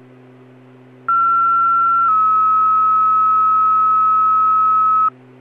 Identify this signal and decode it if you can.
Two-tone page, heard as a one-second tone, then a three-second tone, about 1360 then 1290 Hz. It is heard from 1.1 to 5.1 s.